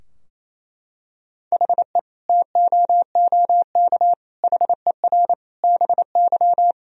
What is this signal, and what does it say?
Morse code, heard as a keyed tone, about 28 wpm, 700 Hz. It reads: HE TOOK HER BY
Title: Sentences from Top 1000 Words - Review Part 2 - 28wpm